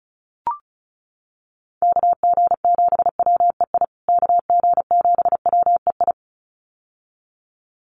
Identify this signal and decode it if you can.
Morse code, heard as a keyed tone, about 34 wpm, 700 Hz. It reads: KG7WEI KG7WEI
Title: USA Call Signs Repeated 2x 35wpm